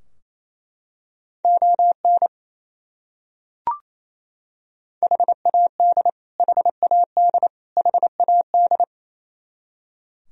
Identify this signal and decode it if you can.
Morse code, heard as a keyed tone, about 28 wpm, 700 Hz. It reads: ON HAD HAD HAD
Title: Top 100 Words Repeated 28wpm